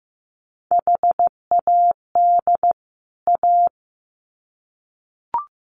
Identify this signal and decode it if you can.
Morse code, heard as a keyed tone, about 15 wpm, 700 Hz. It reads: HAD A
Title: Sets of 2 Words 15wpm